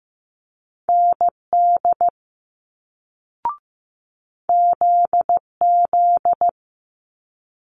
Morse code, heard as a keyed tone, about 15 wpm, 700 Hz. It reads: ND ZZ
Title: Multi-letter Phonemes Quick Fire 15wpm